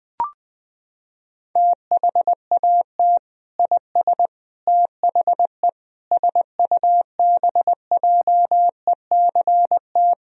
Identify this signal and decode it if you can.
Morse code, heard as a keyed tone, about 21 wpm, 700 Hz. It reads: THAT IS THE SUBJECT
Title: Sentences from Top 1000 Words - Review Part 4 - 20wpm